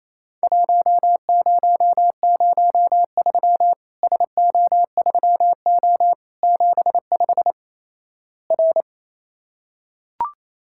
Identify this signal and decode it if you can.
Morse code, heard as a keyed tone, about 28 wpm, 700 Hz. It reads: 1003 SO3O 75 R
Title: Worked All Europe DX Contest - QTC Exchange - 28wpm